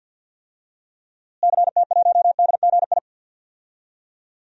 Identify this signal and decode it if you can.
Morse code, heard as a keyed tone, about 46 wpm, 700 Hz. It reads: CT1DGI
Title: DXCC-100 Call Signs - 6 Characters - Repeated 2x - 50wpm